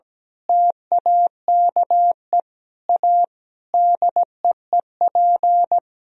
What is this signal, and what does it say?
Morse code, heard as a keyed tone, about 17 wpm, 700 Hz. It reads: TAKE A DEEP
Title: Sentences from Top 1000 Words - Review Part 3 - 17wpm